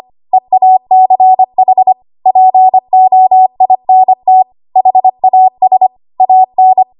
Morse code, heard as a keyed tone, about 25 wpm, 750 Hz. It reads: EACH POINT HAS AN
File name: Archive / 25 WPM / 25WPM